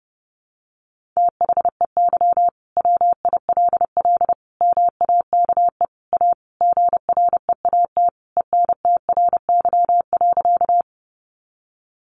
Morse code, heard as a keyed tone, about 30 wpm, 700 Hz. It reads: THEY WILL MAKE A GREAT ENTRY.
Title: Sentences from Top 1500 Words 30wpm